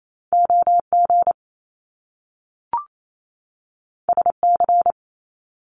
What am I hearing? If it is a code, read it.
Morse code, heard as a keyed tone, about 28 wpm, 700 Hz. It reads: OG SC
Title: Top Two Letter Combinations in Words 28wpm